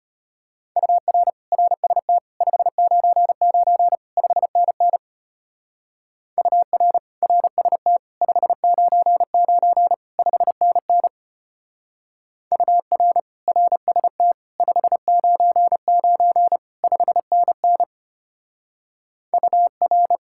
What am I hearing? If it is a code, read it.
Morse code, heard as a keyed tone, about 31 wpm, 700 Hz. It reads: UR RST 599 5NN UR RST 599 5NN UR RST 599 5NN UR
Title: Speed-Racing Common QSO Phrases - 25wpm